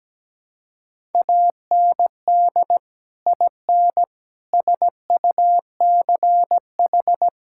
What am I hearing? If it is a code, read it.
Morse code, heard as a keyed tone, about 17 wpm, 700 Hz. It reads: AND IN SUCH
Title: Sentences from Top 1000 Words - Review Part 5 - 17wpm